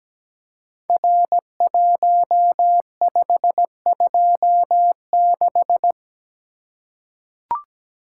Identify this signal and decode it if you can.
Morse code, heard as a keyed tone, about 17 wpm, 700 Hz. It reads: R1526
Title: Code Groups - 5 Letter:Numbers - 17wpm